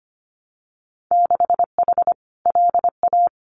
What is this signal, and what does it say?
Morse code, heard as a keyed tone, about 25 wpm, 700 Hz. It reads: 6H LA
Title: Winter Field Day Exchanges Encore 25wpm